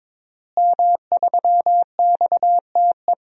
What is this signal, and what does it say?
Morse code, heard as a keyed tone, about 23 wpm, 700 Hz. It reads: M3XTE
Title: DXCC-100 Call Signs - 5 Characters - Repeated 3x - 22wpm